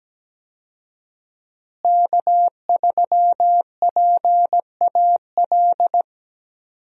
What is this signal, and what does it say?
Morse code, heard as a keyed tone, about 17 wpm, 700 Hz. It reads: K3PAL
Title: USA Call Signs - 5 Characters - Repeated 2x - 17wpm